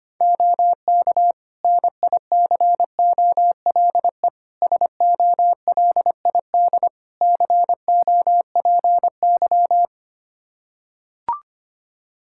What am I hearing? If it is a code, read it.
Morse code, heard as a keyed tone, about 25 wpm, 700 Hz. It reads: OK NICOLE SOLID COPY
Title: Practice QSO - Exchange 3 - More Info - 25wpm